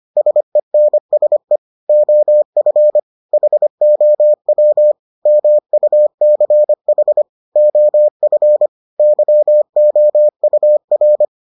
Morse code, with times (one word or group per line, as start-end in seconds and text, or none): 0.2-1.6 SENSE
1.9-3.0 OF
3.3-4.9 HOW
5.3-7.2 MUCH
7.6-8.7 OF
9.0-11.2 YOUR